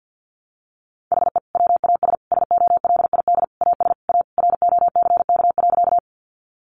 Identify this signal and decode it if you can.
Morse code, heard as a keyed tone, about 50 wpm, 700 Hz. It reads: HE WAS SOLID AS A ROCK.
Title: Sentences from Top 1500 Words 50wpm